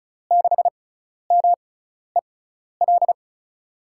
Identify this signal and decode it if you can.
Morse code, heard as a keyed tone, about 36 wpm, spaced as for 12 wpm, 700 Hz. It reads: 6MEL
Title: USA Call Signs - 6 Characters - 6x Character Spacing - 35wpm